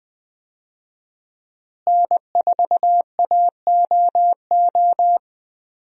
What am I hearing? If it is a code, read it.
Morse code, heard as a keyed tone, about 20 wpm, 700 Hz. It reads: N4AOO
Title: USA Call Signs - 5 Characters - Repeated 2x - 20wpm